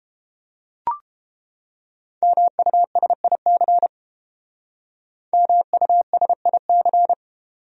Morse code, heard as a keyed tone, about 31 wpm, 700 Hz. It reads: MUSIC MUSIC